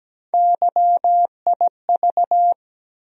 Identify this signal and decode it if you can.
Morse code, heard as a keyed tone, about 17 wpm, 700 Hz. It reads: YIV
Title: USA Call Sign Postfixes - 3 Letters 17wpm